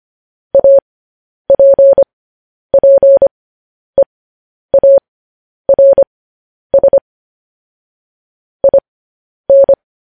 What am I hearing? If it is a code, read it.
Morse code, heard as a keyed tone, about 25 wpm, spaced as for 10 wpm, 550 Hz. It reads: APPEARS IN